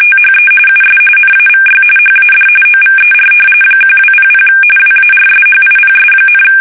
RTTY, about 45.45 baud. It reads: THE US NAVY HAD OVER 1200 WARSHIPS,
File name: fsk850